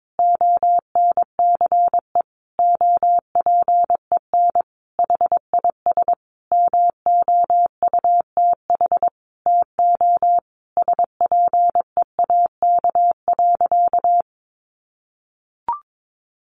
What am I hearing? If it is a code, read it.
Morse code, heard as a keyed tone, about 22 wpm, 700 Hz. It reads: ONCE OPEN HIS MOUTH TO SPEAK.
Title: Sentences from Top 1500 Words 22wpm